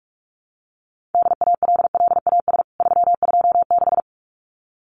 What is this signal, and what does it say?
Morse code, heard as a keyed tone, about 44 wpm, 700 Hz. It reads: DALLAS 326
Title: CWT Exchange 45wpm